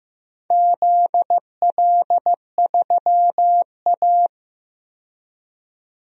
Morse code, heard as a keyed tone, about 15 wpm, 700 Hz. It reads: ZL3A
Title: DXCC-100 Call Signs - 4 Characters - 15wpm